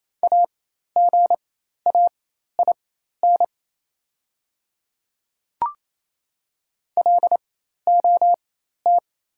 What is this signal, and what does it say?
Morse code, heard as a keyed tone, about 28 wpm, spaced as for 13 wpm, 700 Hz. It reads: AGAIN LOT
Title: 4x Character Spacing - Top 500 Words - 28wpm